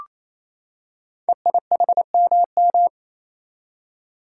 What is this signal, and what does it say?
Morse code, heard as a keyed tone, about 27 wpm, 700 Hz. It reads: EIHMM
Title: Code Groups - 5 Letters - 28wpm